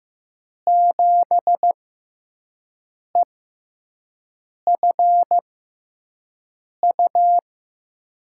Morse code, heard as a keyed tone, about 15 wpm, spaced as for 5 wpm, 700 Hz. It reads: 7EFU
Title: USA Call Signs - 6x Character Spacing - 15wpm